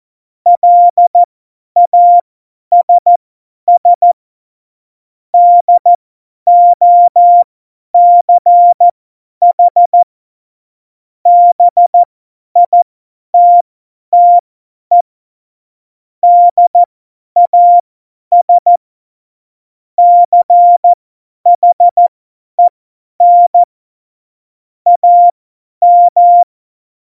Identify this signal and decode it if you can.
Morse code, heard as a keyed tone, about 14 wpm, spaced as for 10 wpm, 700 Hz. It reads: LASS DOCH BITTE DAS CHEN AM
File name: Morsekurs / Witze / Witz 19. Januar 2021, 50 F70